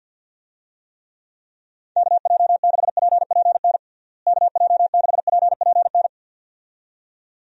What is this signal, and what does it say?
Morse code, heard as a keyed tone, about 47 wpm, 700 Hz. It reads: KJ6PPN KJ6PPN
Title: USA Call Signs - 6 Characters - Repeated 2x - 50wpm